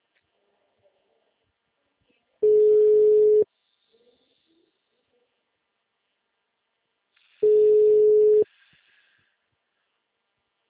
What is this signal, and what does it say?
Special information tone.